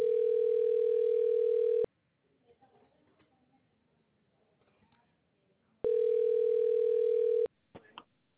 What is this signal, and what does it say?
Ringback tone (North America), 440 + 480 Hz.